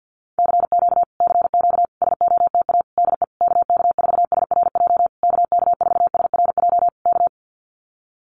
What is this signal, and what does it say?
Morse code, heard as a keyed tone, about 50 wpm, 700 Hz. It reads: CQ CQ SOTA DE KK4SRJ KK4SRJ K
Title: Common QSO Phrases Encore 50wpm